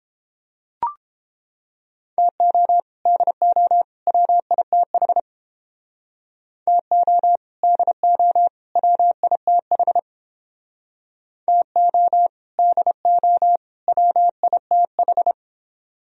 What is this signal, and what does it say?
Morse code, heard as a keyed tone, about 31 wpm, 700 Hz. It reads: TO DO WITH TO DO WITH TO DO WITH